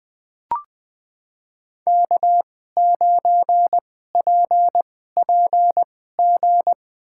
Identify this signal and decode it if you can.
Morse code, heard as a keyed tone, about 20 wpm, spaced as for 14 wpm, 700 Hz. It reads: K9PPG
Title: USA Call Signs - 5 Characters - 2x Character Spacing - 20wpm